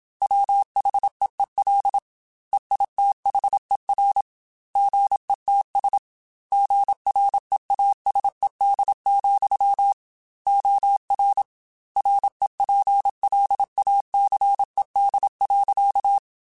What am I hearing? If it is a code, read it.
Morse code, heard as a keyed tone, about 27 wpm, 800 Hz. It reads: WHEEL EITHER GETS GREASED, OR REPLACED.
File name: Chuckles / Chuckles-mp3s / Squeaky @25